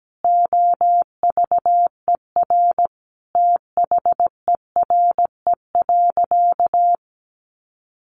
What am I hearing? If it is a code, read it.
Morse code, heard as a keyed tone, about 17 wpm, 700 Hz. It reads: OVER THERE.
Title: Sentences from Top 800 Words 17wpm